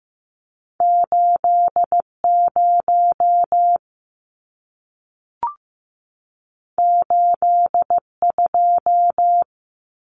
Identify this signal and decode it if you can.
Morse code, heard as a keyed tone, about 15 wpm, 700 Hz. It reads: 80 82
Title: Numbers - 2-Digit 15wpm